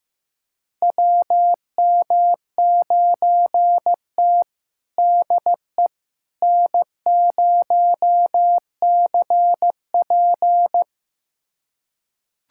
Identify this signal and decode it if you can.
Morse code, heard as a keyed tone, about 15 wpm, 700 Hz. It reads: WM9T DE N0CP
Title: USA Callsign DE Callsign - 4 Characters - 15wpm